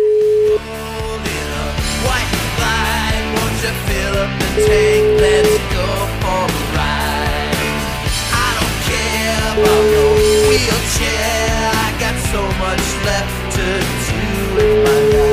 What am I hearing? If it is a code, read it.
Ringback tone (Europe, 425 Hz), with noise.